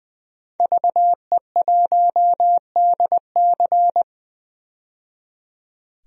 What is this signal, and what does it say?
Morse code, heard as a keyed tone, about 20 wpm, 700 Hz. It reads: VE1DC
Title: DXCC-100 Call Signs - 5 Characters - 20wpm